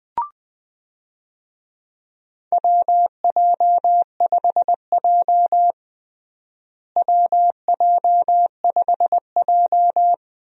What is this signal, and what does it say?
Morse code, heard as a keyed tone, about 21 wpm, 700 Hz. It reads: WJ5J WJ5J
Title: USA Call Signs - 4 Characters - Repeated 2x - 20wpm